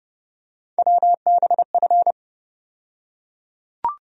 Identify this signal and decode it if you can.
Morse code, heard as a keyed tone, about 31 wpm, 700 Hz. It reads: WBF
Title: USA Call Sign Postfixes - 3 Letters 30wpm